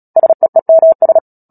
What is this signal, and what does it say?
Morse code, heard as a keyed tone, about 37 wpm, 650 Hz. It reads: SEEMS
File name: Book Two / 10-Chapter 10 - Epilogue - 36 WPM